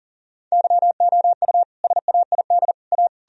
Morse code, heard as a keyed tone, about 40 wpm, 700 Hz. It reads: YOU SAID A
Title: Sentences from Top 1000 Words - Review Part 3 - 40wpm